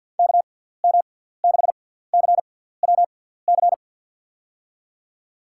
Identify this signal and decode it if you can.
Morse code, heard as a keyed tone, about 52 wpm, spaced as for 17 wpm, 700 Hz. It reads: KM6CWC
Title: USA Call Signs - 6 Characters - 6x Character Spacing - 50wpm